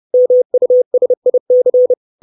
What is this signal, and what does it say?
Morse code, heard as a keyed tone, about 30 wpm, 500 Hz. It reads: MUSIC
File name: 632 Words 30WPM